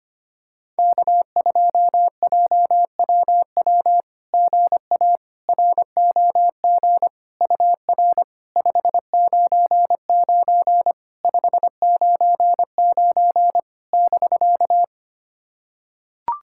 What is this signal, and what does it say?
Morse code, heard as a keyed tone, about 25 wpm, 700 Hz. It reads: K2JWW GA ROG UR 599 599 <BK>
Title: SOTA QSO Practice 25wpm